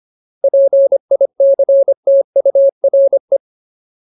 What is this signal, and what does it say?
Morse code, heard as a keyed tone, about 25 wpm, 550 Hz. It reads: PICTURE
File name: Ultimate Word List / 25WPM Ultimate